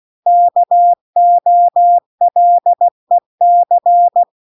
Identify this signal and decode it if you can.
Morse code, heard as a keyed tone, about 16 wpm, 700 Hz. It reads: KOLEC